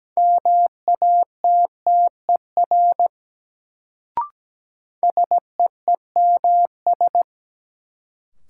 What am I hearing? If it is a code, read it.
Morse code, heard as a keyed tone, about 17 wpm, 700 Hz. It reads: MATTER SEEMS
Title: Sets of 3 Words - New Word Drill - 17wpm